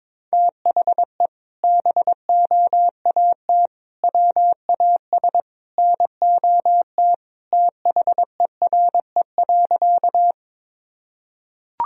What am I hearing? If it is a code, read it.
Morse code, heard as a keyed tone, about 22 wpm, 700 Hz. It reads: THE BOAT WAS NOT THERE.
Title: Sentences from Top 1000 Words - Review Part 3 - 22wpm